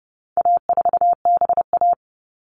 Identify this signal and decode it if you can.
Morse code, heard as a keyed tone, about 30 wpm, 700 Hz. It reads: A4BA
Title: All Cut Numbers - 4-Digit 30wpm